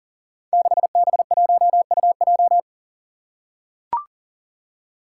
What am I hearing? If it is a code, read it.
Morse code, heard as a keyed tone, about 37 wpm, 700 Hz. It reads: 6B1UJ
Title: Five Challenging Letter-Number Combinations 40wpm